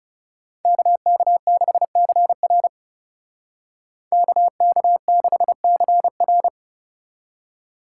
Morse code, heard as a keyed tone, about 36 wpm, 700 Hz. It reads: KK6CR KK6CR
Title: Speed Racing - USA 2x2 Call Signs 30wpm